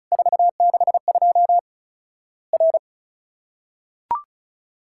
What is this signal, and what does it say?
Morse code, heard as a keyed tone, about 36 wpm, 700 Hz. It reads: 462 R
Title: Worked All Europe DX Contest - QTC Exchange - 35wpm